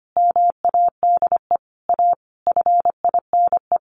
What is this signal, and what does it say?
Morse code, heard as a keyed tone, about 25 wpm, 700 Hz. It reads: MADE A FINE
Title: Sentences from Top 1100 Words 25wpm